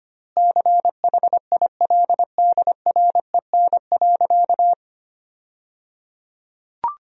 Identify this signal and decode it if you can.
Morse code, heard as a keyed tone, about 25 wpm, 700 Hz. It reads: CHILDREN.
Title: 3x Word-Spacing - Sentences from Top 400 Words - 25wpm